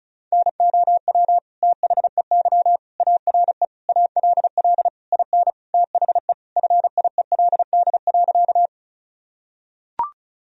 Morse code, heard as a keyed tone, about 36 wpm, 700 Hz. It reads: NOW THEY ARE ALL IN THE FIELD.
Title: Sentences from Top 1000 Words - Review Part 2 - 35wpm